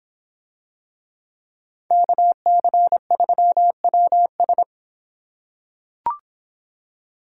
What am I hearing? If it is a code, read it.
Morse code, heard as a keyed tone, about 26 wpm, 700 Hz. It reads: KC3WS